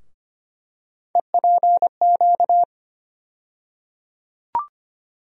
Morse code, heard as a keyed tone, about 25 wpm, 700 Hz. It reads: EPQ
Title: USA Call Sign Postfixes - 3 Letters 25wpm